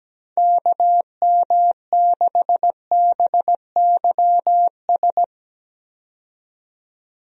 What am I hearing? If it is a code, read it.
Morse code, heard as a keyed tone, about 17 wpm, 700 Hz. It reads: KM6BYS
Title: USA Call Signs - 6 Characters - 17wpm